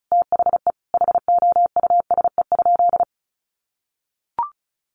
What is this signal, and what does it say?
Morse code, heard as a keyed tone, about 34 wpm, 700 Hz. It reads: THE HOUSE?
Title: Sentences from Top 800 Words 35wpm